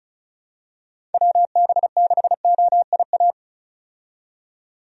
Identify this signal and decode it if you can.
Morse code, heard as a keyed tone, about 36 wpm, 700 Hz. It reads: WB6OIA
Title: USA Call Signs - 6 Characters - 35wpm